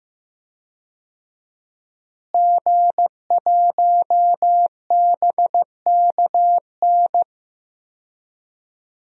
Morse code, heard as a keyed tone, about 15 wpm, 700 Hz. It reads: G1BKN
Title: DXCC-100 Call Signs - 5 Characters - Repeated 3x - 15wpm